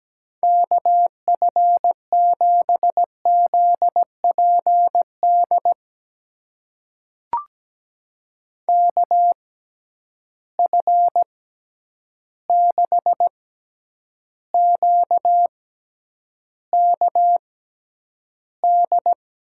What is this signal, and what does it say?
Morse code, heard as a keyed tone, about 17 wpm, spaced as for 9 wpm, 700 Hz. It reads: KF7ZPD KF6QKD
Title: USA Call Signs - 6 Characters - 6x Character Spacing - 17wpm